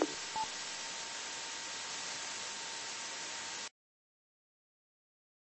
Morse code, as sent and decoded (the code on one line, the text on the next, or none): none